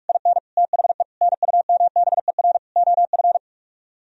Morse code, heard as a keyed tone, about 45 wpm, 700 Hz. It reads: IN THE NUMBER OF